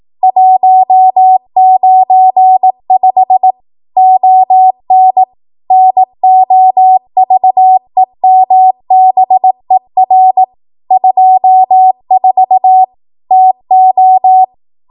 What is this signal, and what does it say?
Morse code, heard as a keyed tone, about 18 wpm, 750 Hz. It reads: 195 ON NOVEMBER 24 TO
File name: Archive / 18 WPM / 18WPM